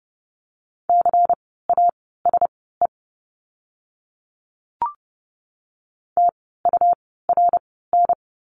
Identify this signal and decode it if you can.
Morse code, heard as a keyed tone, about 30 wpm, spaced as for 17 wpm, 700 Hz. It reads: CASE TURN
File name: Lesson 17f - Letter u - Words 30wpm